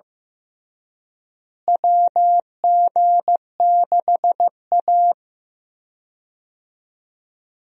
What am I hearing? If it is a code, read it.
Morse code, heard as a keyed tone, about 15 wpm, 700 Hz. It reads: WG6A
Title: USA Call Signs - 4 Characters - Repeated 2x - 15wpm